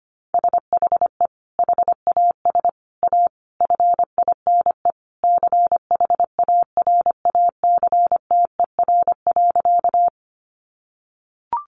Morse code, heard as a keyed tone, about 25 wpm, 700 Hz. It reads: SHE HAS A FINE CHARACTER.
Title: Sentences from Top 1200 Words 25wpm